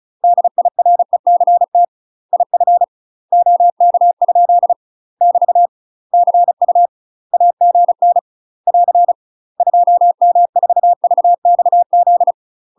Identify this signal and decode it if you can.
Morse code, heard as a keyed tone, about 36 wpm, 700 Hz. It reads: DIRECT IF OK ? <BT> CU AGN <AR> 2M4VXZ
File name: UK MASTER QSOs TRAINING / 35WPM / UK MASTER QSOs 0009